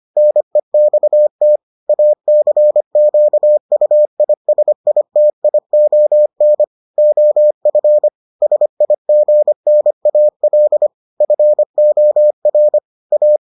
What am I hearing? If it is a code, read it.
Morse code, heard as a keyed tone, about 25 wpm, 600 Hz. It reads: NEXT ACQUISITION OF SIGNAL FOR A